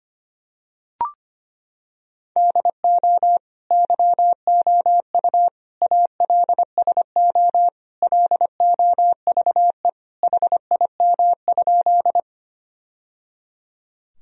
Morse code, heard as a keyed tone, about 25 wpm, 700 Hz. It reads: DO YOU ALSO LOVE HIM?
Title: Sentences from Top 1000 Words - Review Part 1 - 25wpm